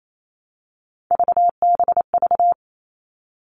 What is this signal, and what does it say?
Morse code, heard as a keyed tone, about 28 wpm, 700 Hz. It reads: VBV
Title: All Cut Numbers - 3-Digit 28wpm